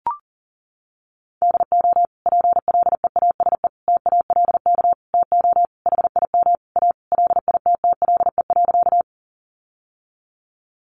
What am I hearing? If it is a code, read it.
Morse code, heard as a keyed tone, about 37 wpm, 700 Hz. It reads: DO PLEASE TALK TO HIM A LITTLE.
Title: Sentences from Top 1500 Words 40wpm